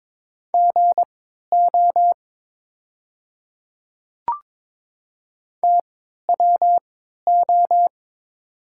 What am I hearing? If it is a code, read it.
Morse code, heard as a keyed tone, about 22 wpm, spaced as for 13 wpm, 700 Hz. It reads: GO TWO
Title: 3x Character Spacing - Top 200 Words - 22wpm